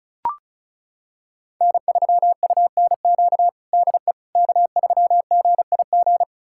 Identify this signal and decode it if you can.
Morse code, heard as a keyed tone, about 36 wpm, 700 Hz. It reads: N3UNQ DE K3GIG
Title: USA Callsign DE Callsign - 5 Characters - 35wpm